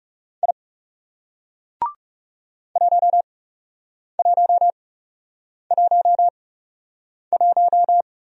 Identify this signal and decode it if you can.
Morse code, heard as a keyed tone, about 40 wpm, spaced as for 8 wpm, 700 Hz. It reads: I 1111